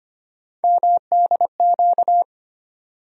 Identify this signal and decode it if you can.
Morse code, heard as a keyed tone, about 25 wpm, 700 Hz. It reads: MDQ
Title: USA Call Sign Postfixes - 3 Letters 25wpm